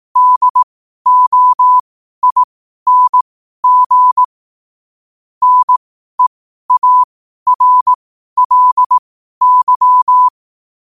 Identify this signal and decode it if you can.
Morse code, heard as a keyed tone, about 18 wpm, spaced as for 12 wpm, 1000 Hz. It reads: DOING NEARLY
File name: qotdfiles / 12wpm